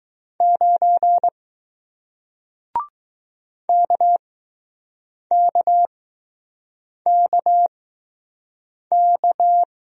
Morse code, heard as a keyed tone, about 20 wpm, spaced as for 6 wpm, 700 Hz. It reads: 9 KKKK